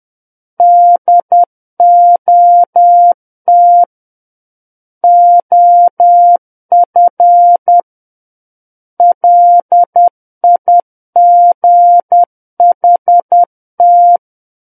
Morse code, heard as a keyed tone, about 10 wpm, 700 Hz. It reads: DOT OF LIGHT
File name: Book One / 01-Chapter 1 - 10 WPM